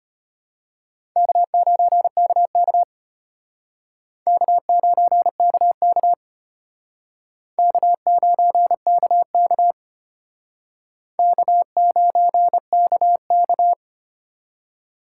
Morse code, heard as a keyed tone, about 33 wpm, 700 Hz. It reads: K9KK K9KK K9KK K9KK